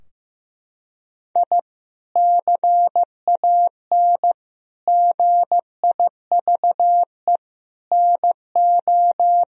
Morse code, heard as a keyed tone, about 15 wpm, 700 Hz. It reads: I CAN GIVE NO
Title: Sentences from Top 1000 Words - Review Part 3 - 15wpm